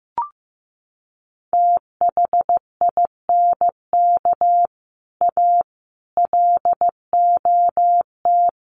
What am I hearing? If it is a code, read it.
Morse code, heard as a keyed tone, about 15 wpm, 700 Hz. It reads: THINK A LOT
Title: Sets of 5 Words 15wpm